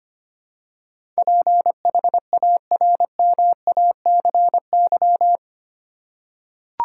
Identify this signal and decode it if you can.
Morse code, heard as a keyed tone, about 25 wpm, 700 Hz. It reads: PHARMACY
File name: Eight Letter Words - 25wpm